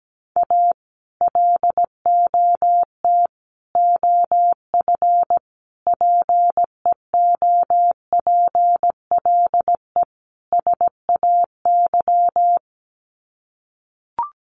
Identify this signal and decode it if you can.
Morse code, heard as a keyed tone, about 17 wpm, 700 Hz. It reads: A LOT OF PEOPLE SAY
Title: Sets of 5 Words Encore2 17wpm